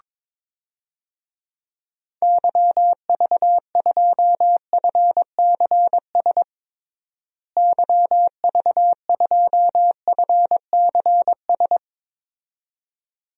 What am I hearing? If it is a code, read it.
Morse code, heard as a keyed tone, about 22 wpm, 700 Hz. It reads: YV2FCS YV2FCS
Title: DXCC-100 Call Signs - 6 Characters - Repeated 2x - 22wpm